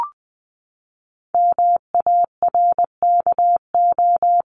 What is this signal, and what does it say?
Morse code, heard as a keyed tone, about 20 wpm, 700 Hz. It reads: MARKO
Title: CWT Exchange 20wpm